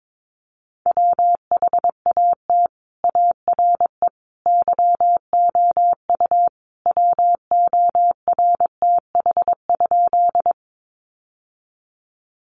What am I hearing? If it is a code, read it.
Morse code, heard as a keyed tone, about 22 wpm, 700 Hz. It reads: WHAT ARE YOU WORTH?
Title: Sentences from Top 800 Words 22wpm